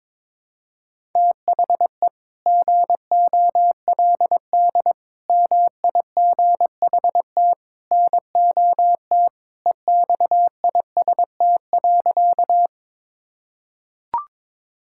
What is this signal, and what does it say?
Morse code, heard as a keyed tone, about 22 wpm, 700 Hz. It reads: THE GOLD MIGHT NOT EXIST.
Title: Sentences from Top 1000 Words - Review Part 5 - 22wpm